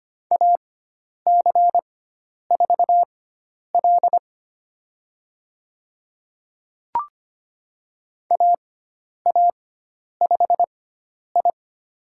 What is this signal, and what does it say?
Morse code, heard as a keyed tone, about 25 wpm, spaced as for 10 wpm, 700 Hz. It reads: AC4L AA5I
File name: Lesson 16g - Letter c - Call Signs 25wpm